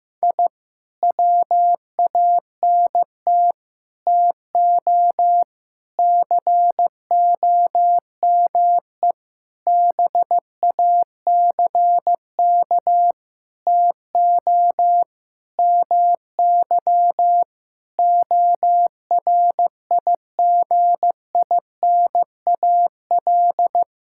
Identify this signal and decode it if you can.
Morse code, heard as a keyed tone, about 15 wpm, 700 Hz. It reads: I WANT TO COME BACK TO MY ORIGINAL
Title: Sentences from Top 1000 Words - Review Part 6 - 15wpm